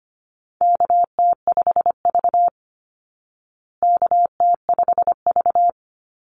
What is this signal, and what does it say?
Morse code, heard as a keyed tone, about 25 wpm, 700 Hz. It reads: KT5V KT5V
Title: Call Signs Repeated 2x - 25wpm